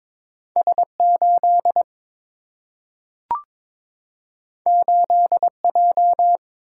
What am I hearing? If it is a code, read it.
Morse code, heard as a keyed tone, about 22 wpm, 700 Hz. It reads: S 8 8 J
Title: Single-Letter Single-Number 22wpm